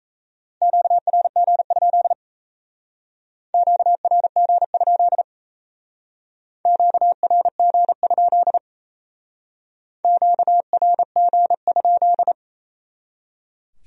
Morse code, heard as a keyed tone, about 35 wpm, 700 Hz. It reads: QRG? QRG? QRG? QRG?
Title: Speed-Racing Q-Codes - 28wpm